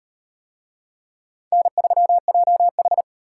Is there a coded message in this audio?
Yes: Morse code from 1.5 to 3.0 s, heard as a keyed tone, about 38 wpm, 700 Hz. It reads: N3JH